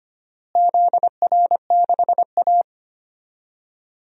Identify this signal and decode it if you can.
Morse code, heard as a keyed tone, about 25 wpm, 700 Hz. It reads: ZR6A
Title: DXCC-100 Call Signs - 4 Characters - Repeated 2x - 25wpm